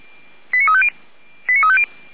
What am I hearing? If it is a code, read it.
Five-tone selective call (ZVEI), heard as quick runs of five tones, tones of about 70 ms. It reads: 89350
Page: /455/SDR/ZVEI/89229/